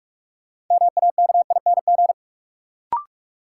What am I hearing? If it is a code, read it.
Morse code, heard as a keyed tone, about 46 wpm, 700 Hz. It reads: MAKING
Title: Sentences from Top 1000 Words - Review Part 6 - New Word Drill - 45wpm